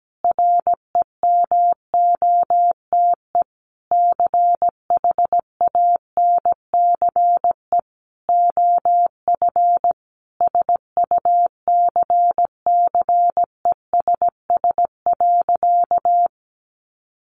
Morse code, heard as a keyed tone, about 17 wpm, 700 Hz. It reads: REMOTE CHANCE OF SUCCESS.
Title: Sentences from Top 1300 Words 17wpm